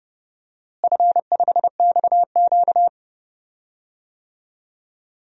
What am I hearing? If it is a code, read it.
Morse code, heard as a keyed tone, about 30 wpm, 700 Hz. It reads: F5XQ
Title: DXCC-100 Call Signs - 4 Characters - Repeated 3x - 30wpm